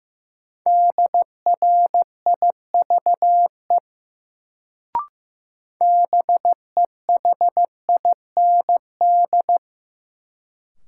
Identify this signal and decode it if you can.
Morse code, heard as a keyed tone, about 15 wpm, 700 Hz. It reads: DRIVE BEHIND
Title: Sentences from Top 500 Words - New Word Drill - 15wpm